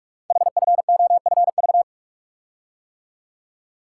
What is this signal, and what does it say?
Morse code, heard as a keyed tone, about 44 wpm, 700 Hz. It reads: HFOFV
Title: Code Groups - 5 Letters - 45wpm